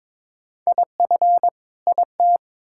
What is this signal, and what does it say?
Morse code, heard as a keyed tone, about 22 wpm, 700 Hz. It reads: IF IT
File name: Build A Phrase - 2-Word Phrases - 22wpm